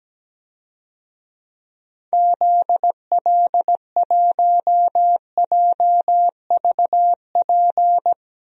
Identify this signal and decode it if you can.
Morse code, heard as a keyed tone, about 17 wpm, 700 Hz. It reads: ZL1JVP
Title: DXCC-100 Call Signs - 6 Characters - Repeated 3x - 17wpm